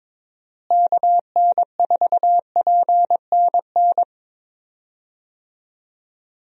Morse code, heard as a keyed tone, about 22 wpm, 700 Hz. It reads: KN4PNN
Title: USA Call Signs - 6 Characters - Repeated 3x - 22wpm